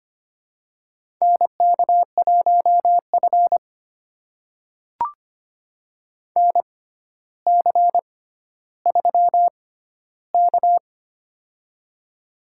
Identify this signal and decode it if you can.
Morse code, heard as a keyed tone, about 25 wpm, spaced as for 13 wpm, 700 Hz. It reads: NK1F NC3K
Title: USA Call Signs - 4 Characters - 6x Character Spacing - 25wpm